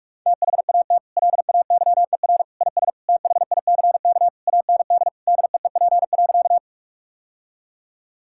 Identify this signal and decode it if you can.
Morse code, heard as a keyed tone, about 41 wpm, 700 Hz. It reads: THAT LAYER IS THICK AND DEEP.
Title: Sentences from Top 1600 Words 45wpm